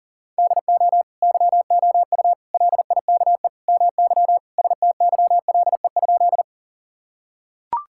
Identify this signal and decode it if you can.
Morse code, heard as a keyed tone, about 40 wpm, 700 Hz. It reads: DO YOU LIKE MY STYLE?
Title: Sentences from Top 1000 Words - Review Part 5 - 40wpm